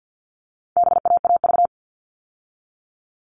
Morse code, heard as a keyed tone, about 50 wpm, 700 Hz. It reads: BAAV
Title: All Cut Numbers - 4-Digit 50wpm